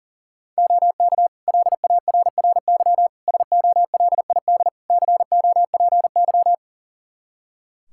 Morse code, heard as a keyed tone, about 40 wpm, 700 Hz. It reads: OK LARRY SOLID COPY
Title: Practice QSO - Exchange 3 - More Info - 40wpm